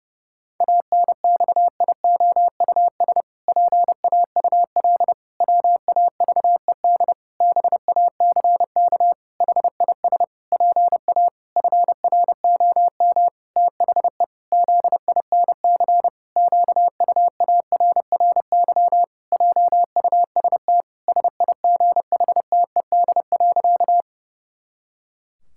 Morse code, heard as a keyed tone, about 30 wpm, 700 Hz. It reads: ANXIOUS PAUL WAVED BACK HIS PA FROM THE ZINC QUARRY JUST SIGHTED.
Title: Pangram Challenge v2 - 30wpm